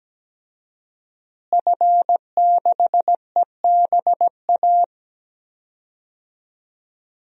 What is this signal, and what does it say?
Morse code, heard as a keyed tone, about 17 wpm, 700 Hz. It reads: F6EBA
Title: DXCC-100 Call Signs - 5 Characters - Repeated 3x - 17wpm